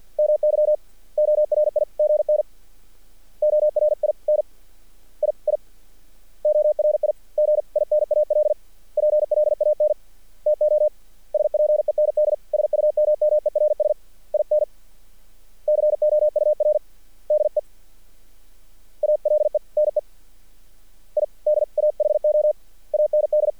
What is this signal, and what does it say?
Morse code, heard as a keyed tone, about 48 wpm, 600 Hz. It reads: MY ORIGN ORIN I I ORIGINAL PLAN TO SPEND SUMMERS IN COUR DE ALENE IDAHO AND
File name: archived / CRSnet